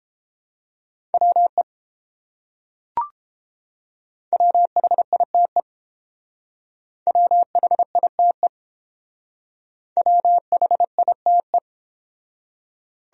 Morse code, heard as a keyed tone, about 28 wpm, 700 Hz. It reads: WE WHITE WHITE WHITE